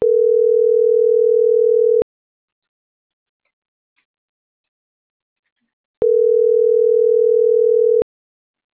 Ringback tone (North America), 440 + 480 Hz.